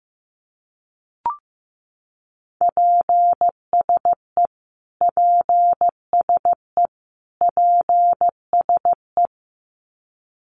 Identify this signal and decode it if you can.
Morse code, heard as a keyed tone, about 15 wpm, 700 Hz. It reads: PSE PSE PSE
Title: Abbreviations Repeated 3x - 15wpm